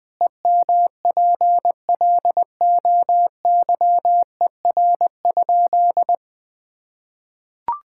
Morse code, heard as a keyed tone, about 21 wpm, 700 Hz. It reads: EMPLOYER?
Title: Sentences from Top 1600 Words 20wpm